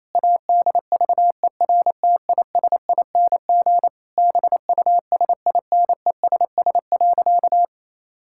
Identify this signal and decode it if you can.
Morse code, heard as a keyed tone, about 28 wpm, 700 Hz. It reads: ADVERTISING BUSINESS.
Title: Sentences from Top 1000 Words - Review Part 6 - 28wpm